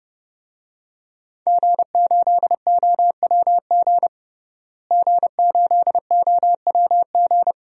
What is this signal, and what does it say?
Morse code, heard as a keyed tone, about 31 wpm, 700 Hz. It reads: G8OWG G8OWG
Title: DXCC-100 Call Signs - 5 Characters - Repeated 3x - 30wpm